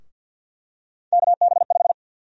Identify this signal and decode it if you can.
Morse code, heard as a keyed tone, about 52 wpm, 700 Hz. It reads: KB5
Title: USA Call Sign Prefixes - 50wpm